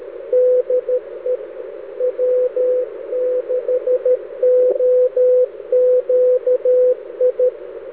Morse code, heard as a keyed tone, about 13 wpm, 500 Hz. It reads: DE W6OQI
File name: W6OQI JAN13-2010-80M-SHACK-snip